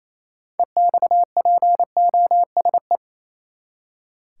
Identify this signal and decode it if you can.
Morse code, heard as a keyed tone, about 28 wpm, 700 Hz. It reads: EXPOSE
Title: Six Letter Words - 28wpm